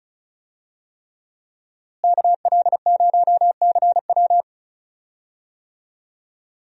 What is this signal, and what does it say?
Morse code, heard as a keyed tone, about 36 wpm, 700 Hz. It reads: KL0CW
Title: USA Call Signs - 5 Characters - Repeated 3x - 35wpm